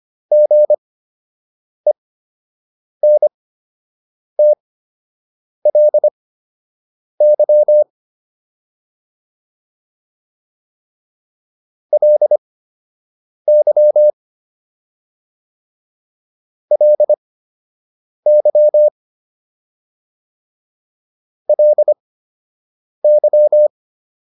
Morse code, heard as a keyed tone, about 25 wpm, spaced as for 7 wpm, 600 Hz. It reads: GENTLY LY LY LY